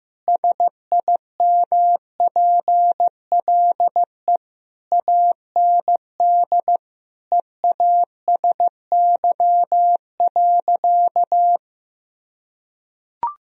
Morse code, heard as a keyed tone, about 15 wpm, 700 Hz. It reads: SIMPLE AND EASY.
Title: Sentences from Top 1000 Words - Review Part 3 - 15wpm